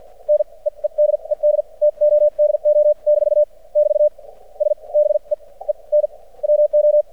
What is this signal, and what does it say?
Morse code, heard as a keyed tone, about 53 wpm, spaced as for 35 wpm, 600 Hz. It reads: NEEDED TODO - <BT> S BEEN WO
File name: archived / CRSnet